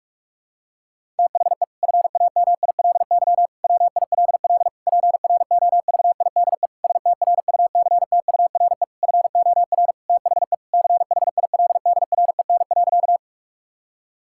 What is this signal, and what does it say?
Morse code, heard as a keyed tone, about 46 wpm, 700 Hz. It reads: THE FAMILY WILL PROVIDE STRUCTURE FOR THE CHILDREN.
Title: Sentences from Top 1000 Words - Review Part 6 - 45wpm